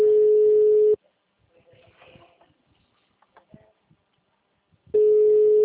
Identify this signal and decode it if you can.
Ringback tone (Europe, 425 Hz), with speech.